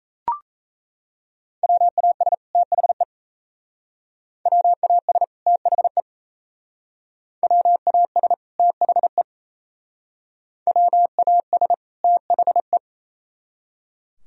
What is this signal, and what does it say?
Morse code, heard as a keyed tone, about 35 wpm, 700 Hz. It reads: WAS THE WAS THE WAS THE WAS THE
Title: Speed-Racing - Sets of 2 Words 28wpm